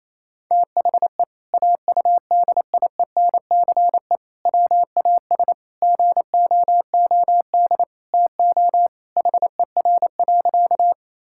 Morse code, heard as a keyed tone, about 28 wpm, 700 Hz. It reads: THE AUDIENCE WAS GOOD TO HER.
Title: Sentences from Top 1000 Words - Review Part 5 - 28wpm